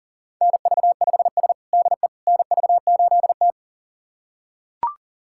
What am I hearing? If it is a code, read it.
Morse code, heard as a keyed tone, about 40 wpm, 700 Hz. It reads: NV5S DE NV8T
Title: USA Callsign DE Callsign - 4 Characters - 40wpm